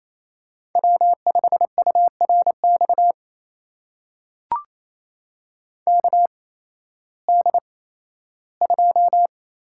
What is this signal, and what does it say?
Morse code, heard as a keyed tone, about 28 wpm, spaced as for 15 wpm, 700 Hz. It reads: W5URX KD2
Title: USA Call Signs - 8x Character Spacing - 28wpm